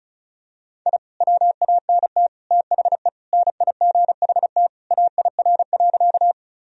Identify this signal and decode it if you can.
Morse code, heard as a keyed tone, about 36 wpm, 700 Hz. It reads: I WANT THE NIGHT AIR.
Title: Sentences from Top 1000 Words - Review Part 1 - 35wpm